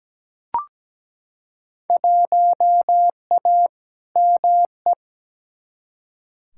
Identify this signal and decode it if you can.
Morse code, heard as a keyed tone, about 17 wpm, 700 Hz. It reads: 1A ME
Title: Field Day Exchanges Encore - 17wpm